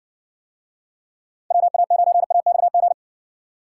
Morse code, heard as a keyed tone, about 58 wpm, 700 Hz. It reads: WA9AXG